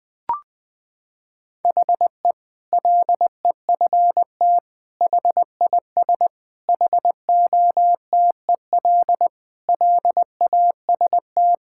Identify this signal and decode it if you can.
Morse code, heard as a keyed tone, about 20 wpm, 700 Hz. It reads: HE LEFT HIS HOTEL LAST
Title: Sentences from Top 1600 Words 20wpm